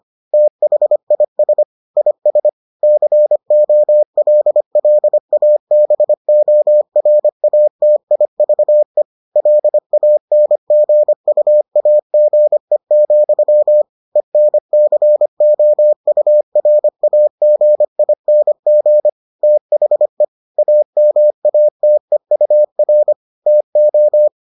Morse code, heard as a keyed tone, about 25 wpm, 600 Hz. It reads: THIS IS COLLABORATIVE LANGUAGE, ENCOURAGING THE AMATEUR TO